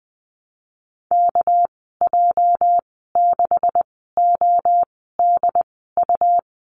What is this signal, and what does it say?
Morse code, heard as a keyed tone, about 20 wpm, spaced as for 14 wpm, 700 Hz. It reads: KJ6ODU
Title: 213 Lesson 34g - Number 8 - Call Signs 20wpm